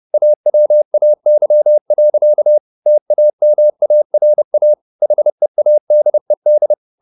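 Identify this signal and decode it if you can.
Morse code, heard as a keyed tone, about 30 wpm, 600 Hz. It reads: AWAY. TAMARA HEADED
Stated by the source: Morse code at 30 wpm, the novel Never by Ken Follett in English, from CW med Gnister.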